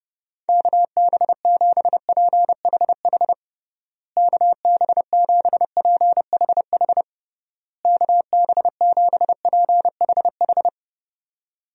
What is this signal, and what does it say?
Morse code, heard as a keyed tone, about 30 wpm, 700 Hz. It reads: KB7PHH KB7PHH KB7PHH
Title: USA Call Signs - 6 Characters - Repeated 3x - 30wpm